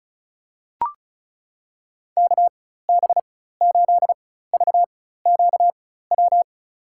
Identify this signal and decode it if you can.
Morse code, heard as a keyed tone, about 34 wpm, spaced as for 16 wpm, 700 Hz. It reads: KB8VQW
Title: USA Call Signs - 6 Characters - 4x Character Spacing - 35wpm